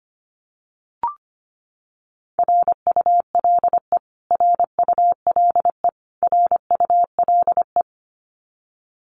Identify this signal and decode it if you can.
Morse code, heard as a keyed tone, about 25 wpm, 700 Hz. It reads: RULE RULE RULE
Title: Top 500 Words Repeated 25wpm